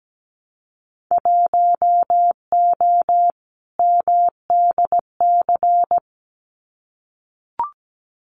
Morse code, heard as a keyed tone, about 17 wpm, 700 Hz. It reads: 1O MDC
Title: Winter Field Day Exchanges 17wpm